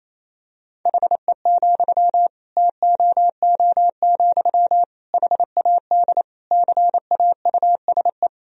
Morse code, heard as a keyed tone, about 28 wpm, 700 Hz. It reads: HE, TOO, HAD CAUSE
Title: Sentences from Top 1000 Words - Review Part 3 - 28wpm